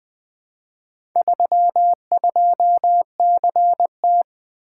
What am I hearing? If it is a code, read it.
Morse code, heard as a keyed tone, about 20 wpm, 700 Hz. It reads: 32CT
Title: Four Letter-Number Combinations - 20wpm